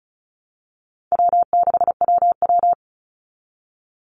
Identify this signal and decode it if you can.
Morse code, heard as a keyed tone, about 34 wpm, 700 Hz. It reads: W6WW
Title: NA SOTA Chaser Callsigns 35wpm